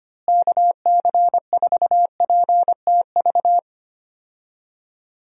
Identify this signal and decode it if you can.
Morse code, heard as a keyed tone, about 25 wpm, 700 Hz. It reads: KC4PTV
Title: USA Call Signs - 6 Characters - Repeated 2x - 25wpm